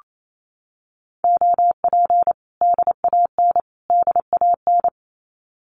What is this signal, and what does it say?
Morse code, heard as a keyed tone, about 28 wpm, 700 Hz. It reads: OP DAN DAN
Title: Common QSO Phrases Encore 28wpm